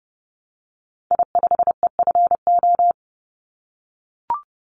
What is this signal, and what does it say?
Morse code, heard as a keyed tone, about 30 wpm, 700 Hz. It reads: I5EFO
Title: Call Signs - 30wpm